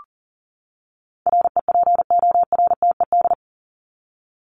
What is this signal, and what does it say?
Morse code, heard as a keyed tone, about 40 wpm, 700 Hz. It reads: REPORTED
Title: ED Suffix - Top 100 Words with Base Words 40wpm